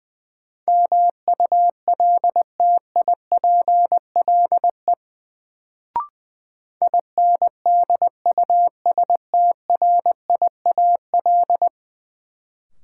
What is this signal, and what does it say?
Morse code, heard as a keyed tone, about 20 wpm, 700 Hz. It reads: MULTIPLE INDUSTRIAL
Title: Sets of 5 Words - New Word Drill - 20wpm